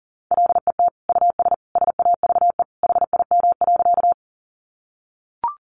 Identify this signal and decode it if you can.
Morse code, heard as a keyed tone, about 40 wpm, 700 Hz. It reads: LET US SAVE HIM.
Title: Sentences from Top 1100 Words 40wpm